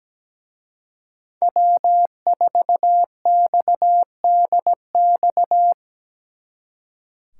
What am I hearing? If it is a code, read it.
Morse code, heard as a keyed tone, about 20 wpm, 700 Hz. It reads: W4XDX
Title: Speed Racing - USA 1x3 Call Signs 17wpm